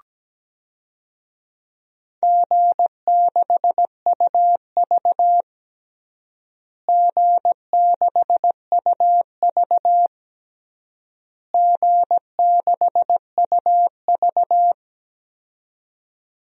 Morse code, heard as a keyed tone, about 17 wpm, 700 Hz. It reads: G6UV G6UV G6UV
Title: DXCC-100 Call Signs - 4 Characters - Repeated 3x - 17wpm